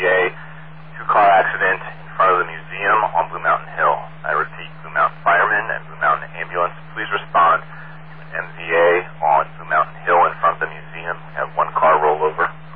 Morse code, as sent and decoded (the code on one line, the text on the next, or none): none